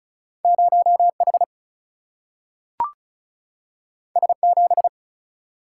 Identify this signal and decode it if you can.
Morse code, heard as a keyed tone, about 36 wpm, 700 Hz. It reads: 0H S7
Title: Single-Letter Single-Number 35wpm